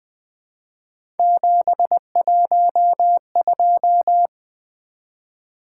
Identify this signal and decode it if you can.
Morse code, heard as a keyed tone, about 20 wpm, 700 Hz. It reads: 712
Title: Three Letter-Number Combinations - 20wpm